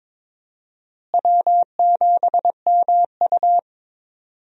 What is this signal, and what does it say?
Morse code, heard as a keyed tone, about 23 wpm, 700 Hz. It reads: W7MU
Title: USA Call Signs - 4 Characters - Repeated 2x - 22wpm